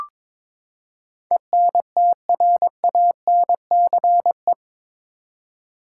Morse code, heard as a keyed tone, about 22 wpm, 700 Hz. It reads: ENTRANCE
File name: practice / 22wpm / Eight Letter Words - 22wpm